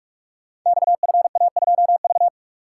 Morse code, heard as a keyed tone, about 41 wpm, 700 Hz. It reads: XFA2V
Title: Code Groups - 5 Letter:Numbers - 45wpm